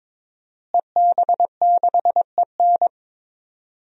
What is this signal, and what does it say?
Morse code, heard as a keyed tone, about 21 wpm, 700 Hz. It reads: EB6EN
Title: DXCC-100 Call Signs - 5 Characters - Repeated 2x - 22wpm